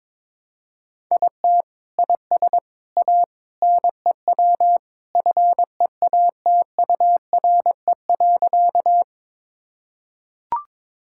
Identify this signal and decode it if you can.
Morse code, heard as a keyed tone, about 22 wpm, 700 Hz. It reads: IT IS A NEW FEATURE.
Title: Sentences from Top 1000 Words - Review Part 5 - 22wpm